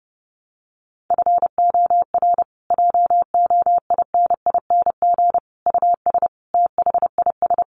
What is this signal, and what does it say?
Morse code, heard as a keyed tone, about 30 wpm, 700 Hz. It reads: FOR JOINING US THIS
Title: Sets of 5 Words Encore 30wpm